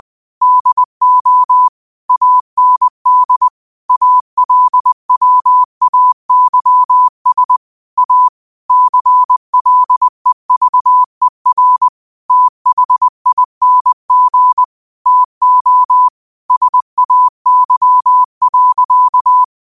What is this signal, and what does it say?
Morse code, heard as a keyed tone, about 20 wpm, 1000 Hz. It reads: DO AND ALWAYS A CLEVER THING TO SAY.
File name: qotdfiles / 20wpm